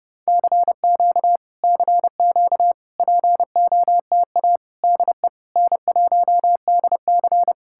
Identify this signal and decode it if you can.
Morse code, heard as a keyed tone, about 33 wpm, 700 Hz. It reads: CQ CQ POTA DE N1DC